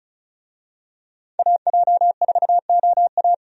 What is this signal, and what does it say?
Morse code, heard as a keyed tone, about 36 wpm, 700 Hz. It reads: AJ4OA
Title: USA Call Signs - 5 Characters - 35wpm